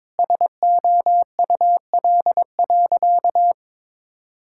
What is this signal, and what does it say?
Morse code, heard as a keyed tone, about 22 wpm, 700 Hz. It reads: SOUL.
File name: Difficult Sentences with Hints - 22wpm